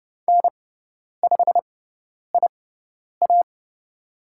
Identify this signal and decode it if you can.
Morse code, heard as a keyed tone, about 29 wpm, spaced as for 10 wpm, 700 Hz. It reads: N5IA
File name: Lesson 17g - Letter u - Call Signs 30wpm